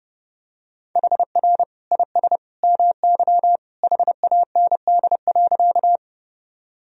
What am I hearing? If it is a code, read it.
Morse code, heard as a keyed tone, about 30 wpm, 700 Hz. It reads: HR IS MY HAND.
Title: Sentences from Top 200 Words and Abbreviations 30wpm